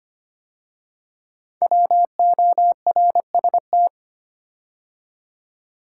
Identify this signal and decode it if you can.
Morse code, heard as a keyed tone, about 25 wpm, 700 Hz. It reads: WORST
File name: 5 Letter Words - First 3 Letters Given - 25wpm